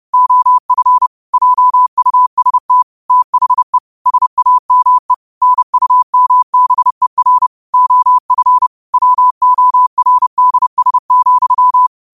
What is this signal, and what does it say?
Morse code, heard as a keyed tone, about 30 wpm, 1000 Hz. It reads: OF JUST THE SAME NUMBER OF WORDS,
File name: qotdfiles / 30wpm